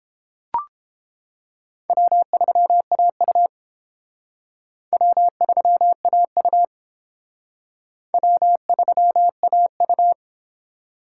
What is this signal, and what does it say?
Morse code, heard as a keyed tone, about 29 wpm, 700 Hz. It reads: W3AU W3AU W3AU